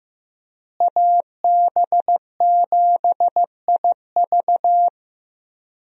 Morse code, heard as a keyed tone, about 15 wpm, 700 Hz. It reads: AB7IV
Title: USA Call Signs - 5 Characters - Repeated 3x - 15wpm